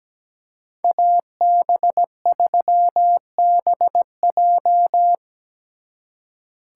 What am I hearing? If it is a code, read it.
Morse code, heard as a keyed tone, about 20 wpm, 700 Hz. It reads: AB3BJ
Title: Speed Racing - USA 2x2 Call Signs 17wpm